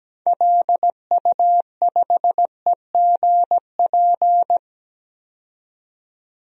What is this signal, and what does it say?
Morse code, heard as a keyed tone, about 17 wpm, 700 Hz. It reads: LU5EGP
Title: DXCC-100 Call Signs - 6 Characters - 17wpm